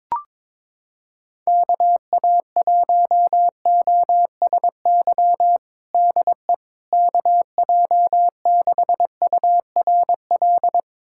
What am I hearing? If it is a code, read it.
Morse code, heard as a keyed tone, about 22 wpm, 700 Hz. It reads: KA1OSY DE KJ6URL
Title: USA Callsign DE Callsign - 6 Characters - 22wpm